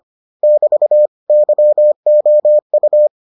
Morse code, heard as a keyed tone, about 25 wpm, 600 Hz. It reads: <BT> YOU